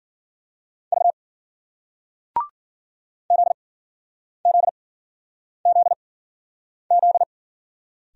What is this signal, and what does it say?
Morse code, heard as a keyed tone, about 56 wpm, spaced as for 9 wpm, 700 Hz. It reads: V ZZZZ